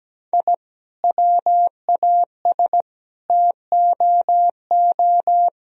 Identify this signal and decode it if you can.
Morse code, heard as a keyed tone, about 17 wpm, 700 Hz. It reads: I WAS TOO
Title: Sentences from Top 1000 Words - Review Part 4 - 17wpm